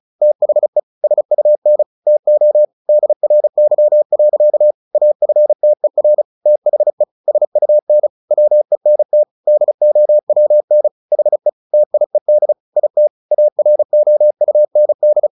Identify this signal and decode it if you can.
Morse code, heard as a keyed tone, about 34 wpm, 600 Hz. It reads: THE SUN TO DRY. AFTER THE SUN WENT DOWN HE TIED IT AROUND
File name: the old man and the sea / the old man and the sea 1 35wpm